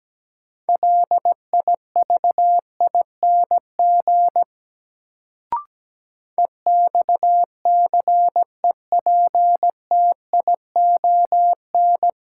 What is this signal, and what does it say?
Morse code, heard as a keyed tone, about 17 wpm, 700 Hz. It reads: LIVING EXCEPTION
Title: Sets of 4 Words - New Word Drill - 17wpm